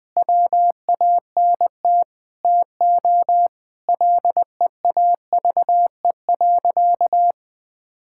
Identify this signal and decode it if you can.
Morse code, heard as a keyed tone, about 20 wpm, 700 Hz. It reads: WANT TO LEAVE.
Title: Sentences with 5-Word Phrases - 20wpm